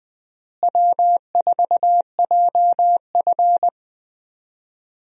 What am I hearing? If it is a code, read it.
Morse code, heard as a keyed tone, about 20 wpm, 700 Hz. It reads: W4JF
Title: USA Call Signs - 4 Characters - Repeated 3x - 20wpm